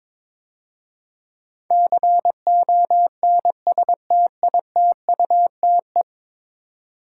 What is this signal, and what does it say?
Morse code, heard as a keyed tone, about 22 wpm, 700 Hz. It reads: CONSTITUTE
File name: 10 Letter Words - First 3 Letters Given - 22wpm